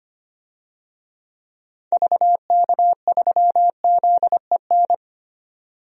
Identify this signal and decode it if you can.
Morse code, heard as a keyed tone, about 25 wpm, 700 Hz. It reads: VK3ZEN
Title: DXCC-100 Call Signs - 6 Characters - 25wpm